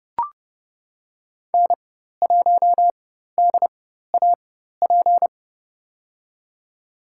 Morse code, heard as a keyed tone, about 30 wpm, spaced as for 14 wpm, 700 Hz. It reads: N1DAP
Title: USA Call Signs - 4x Character Spacing - 30wpm